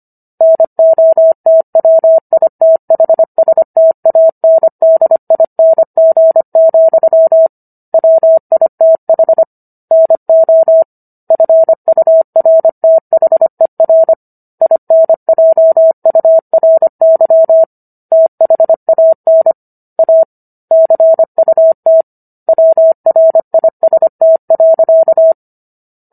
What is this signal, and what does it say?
Morse code, heard as a keyed tone, about 25 wpm, 650 Hz. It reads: NOTWITHSTANDING, WITH NO FURTHER INJURY THAN A CUT WRIST.
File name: Book One / 16-Chapter 16 - 25 WPM